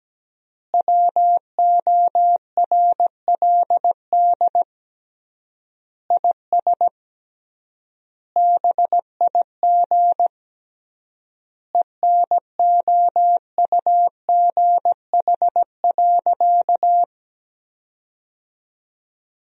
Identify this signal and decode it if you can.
Morse code, heard as a keyed tone, about 17 wpm, 700 Hz. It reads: WORLD IS BIG ENOUGH.
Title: 3x Word-Spacing - Sentences from Top 400 Words - 17wpm